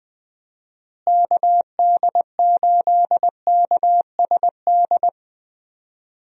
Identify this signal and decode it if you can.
Morse code, heard as a keyed tone, about 20 wpm, 700 Hz. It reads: KD8KSD
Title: USA Call Signs - 6 Characters - Repeated 2x - 20wpm